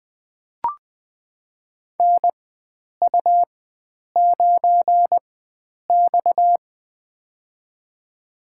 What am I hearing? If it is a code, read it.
Morse code, heard as a keyed tone, about 20 wpm, spaced as for 9 wpm, 700 Hz. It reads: NU9X
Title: USA Call Signs - 4x Character Spacing - 20wpm